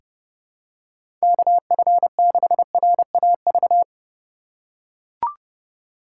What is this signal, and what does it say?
Morse code, heard as a keyed tone, about 30 wpm, 700 Hz. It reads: KF6RAV
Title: USA Call Signs - 6 Characters - 6x Character Spacing - 30wpm